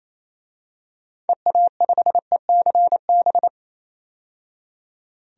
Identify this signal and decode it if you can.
Morse code, heard as a keyed tone, about 28 wpm, 700 Hz. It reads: EA5ECB
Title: DXCC-100 Call Signs - 6 Characters - 28wpm